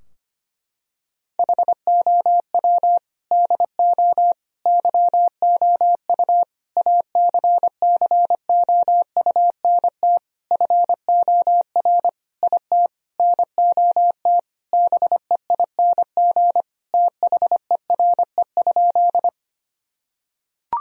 Morse code, heard as a keyed tone, about 25 wpm, 700 Hz. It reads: HOW DO YOU ACCOUNT FOR IT NOT BEING THERE?
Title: Sentences from Top 1000 Words - Review Part 3 - 25wpm